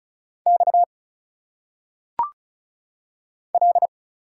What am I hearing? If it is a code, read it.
Morse code, heard as a keyed tone, about 36 wpm, 700 Hz. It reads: X L
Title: Single Letters - Phonetic Pronunciation 35wpm